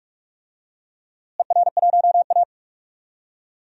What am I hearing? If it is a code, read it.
Morse code, heard as a keyed tone, about 46 wpm, 700 Hz. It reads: ER1A
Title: DXCC-100 Call Signs - 4 Characters - Repeated 3x - 45wpm